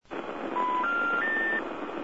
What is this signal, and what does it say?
Special information tone, with noise.
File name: attngt